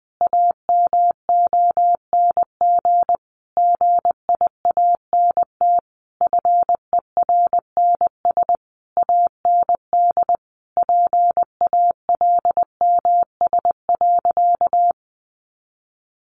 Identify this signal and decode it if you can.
Morse code, heard as a keyed tone, about 20 wpm, 700 Hz. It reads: AMONG GIANT FERNS AND PALMS.
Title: A Sound Of Thunder 20wpm